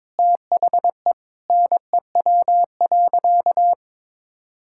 Morse code, heard as a keyed tone, about 21 wpm, 700 Hz. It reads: THE NEW.
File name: Sentences with 9-Word Phrases - 22wpm